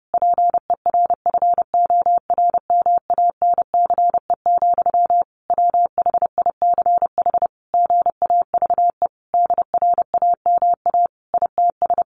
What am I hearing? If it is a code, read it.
Morse code, heard as a keyed tone, about 30 wpm, 700 Hz. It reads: PERFORMANCE, WHICH GAVE DRAMA ITS
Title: Sentences from Top 1500 Words 30wpm